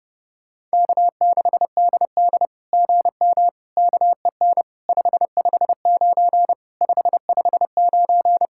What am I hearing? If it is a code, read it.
Morse code, heard as a keyed tone, about 30 wpm, 700 Hz. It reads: K6DD GM KEN 559 559
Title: SOTA QSO Practice 30wpm